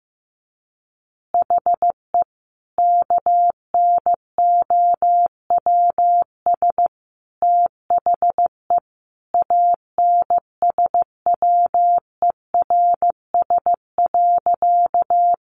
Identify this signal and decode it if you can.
Morse code, heard as a keyed tone, about 15 wpm, 700 Hz. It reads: HE KNOWS THE ANSWERS.
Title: Additional Easy Phrases 15wpm